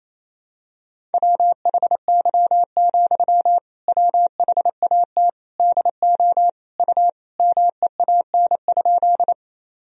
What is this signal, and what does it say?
Morse code, heard as a keyed tone, about 28 wpm, 700 Hz. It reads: WHY, WHAT DO U MEAN?
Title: Sentences from Top 200 Words and Abbreviations 28wpm